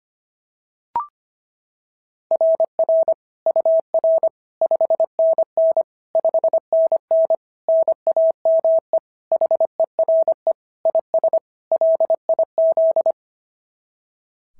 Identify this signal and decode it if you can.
Morse code, heard as a keyed tone, about 24 wpm, 650 Hz. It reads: RR UR 5NN 5NN NAME HERE IS LIZ
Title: SOTA QSO Practice 25wpm